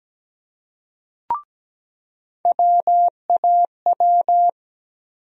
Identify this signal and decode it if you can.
Morse code, heard as a keyed tone, about 17 wpm, 700 Hz. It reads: WAW
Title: USA Call Sign Postfixes - 3 Letters 17wpm